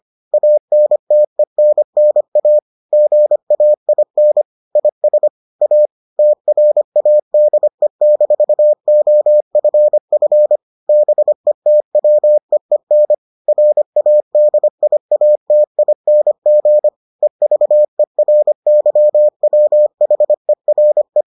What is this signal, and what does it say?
Morse code, heard as a keyed tone, about 25 wpm, 600 Hz. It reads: ANTENNA GAIN IS A TRADE-OFF BETWEEN RADIATING EVERYWHERE